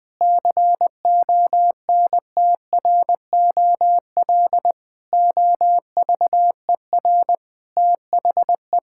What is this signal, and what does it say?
Morse code, heard as a keyed tone, about 21 wpm, 700 Hz. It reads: CONTROL OVER THE
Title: Sentences from Top 1000 Words - Review Part 5 - 20wpm